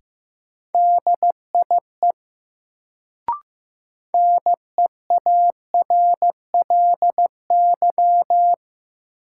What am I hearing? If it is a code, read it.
Morse code, heard as a keyed tone, about 15 wpm, 700 Hz. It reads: DIE NEARLY
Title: Sentences from Top 1300 Words - New Word Drill Alt - 15wpm